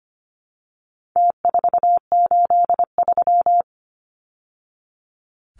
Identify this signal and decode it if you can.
Morse code, heard as a keyed tone, about 25 wpm, 700 Hz. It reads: T483
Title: Numbers - Contest Style - 4-Digit 25wpm